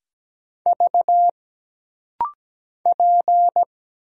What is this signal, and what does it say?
Morse code, heard as a keyed tone, about 17 wpm, 700 Hz. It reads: V P
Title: Single Letter-Number Alt - Rapid-Fire 17wpm